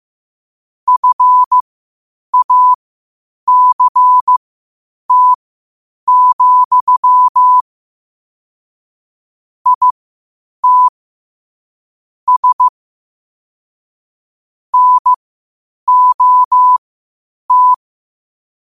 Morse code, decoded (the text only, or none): FACT, IT S NOT